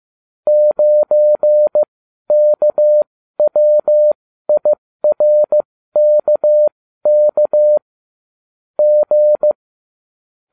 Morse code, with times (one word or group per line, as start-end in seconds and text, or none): 0.5-7.8 9KWIRKK
8.8-9.5 G